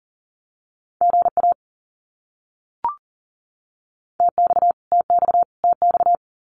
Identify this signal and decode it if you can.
Morse code, heard as a keyed tone, about 40 wpm, 700 Hz. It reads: GA TX TX TX
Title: Abbreviations Repeated 3x - 40wpm